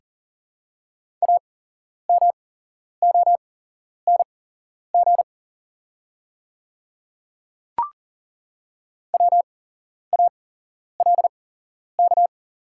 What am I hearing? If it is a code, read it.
Morse code, heard as a keyed tone, about 37 wpm, spaced as for 11 wpm, 700 Hz. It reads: AMONG WALK
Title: 8x Character Spacing - Top 500 Words - 40wpm